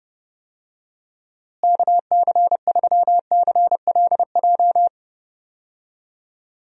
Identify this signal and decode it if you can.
Morse code, heard as a keyed tone, about 30 wpm, 700 Hz. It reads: KC3CLJ
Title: USA Call Signs - 6 Characters - Repeated 3x - 30wpm